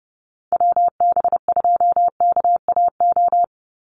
Morse code, heard as a keyed tone, about 30 wpm, 700 Hz. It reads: WB2KAO
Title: Call Signs Repeated 3x - 30wpm